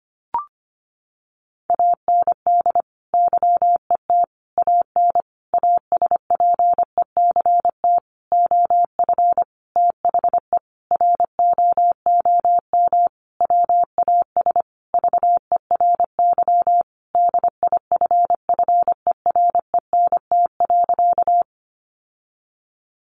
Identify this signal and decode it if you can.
Morse code, heard as a keyed tone, about 25 wpm, 700 Hz. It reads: AND YET AN ASPECT OF THE ROOM WAS VERY DIFFERENT.
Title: Sentences from Top 1300 Words 25wpm